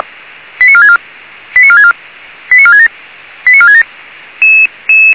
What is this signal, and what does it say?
Five-tone selective call (ZVEI), heard as quick runs of five tones, tones of about 70 ms. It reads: 89464 89464 89467 89467